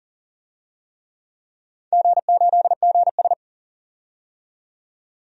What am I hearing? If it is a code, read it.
Morse code, heard as a keyed tone, about 43 wpm, 690 Hz. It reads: G8GS
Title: DXCC-100 Call Signs - 4 Characters - 40wpm